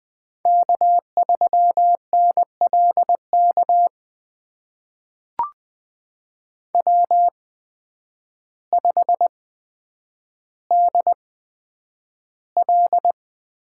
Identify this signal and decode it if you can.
Morse code, heard as a keyed tone, about 20 wpm, spaced as for 9 wpm, 700 Hz. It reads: K3NLK W5DL
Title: USA Call Signs - 5 Characters - 8x Character Spacing - 20wpm